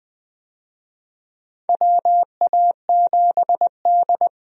Morse code, heard as a keyed tone, about 21 wpm, 700 Hz. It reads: WA7D
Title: USA Call Signs - 4 Characters - 20wpm